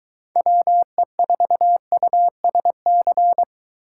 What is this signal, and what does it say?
Morse code, heard as a keyed tone, about 22 wpm, 700 Hz. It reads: WE4USC